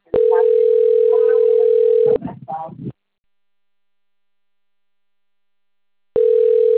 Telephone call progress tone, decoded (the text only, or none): ringback tone